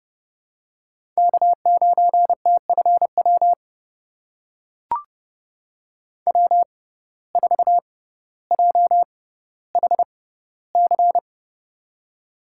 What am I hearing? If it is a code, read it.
Morse code, heard as a keyed tone, about 30 wpm, spaced as for 15 wpm, 700 Hz. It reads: K9TFW W4JHC
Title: USA Call Signs - 6x Character Spacing - 30wpm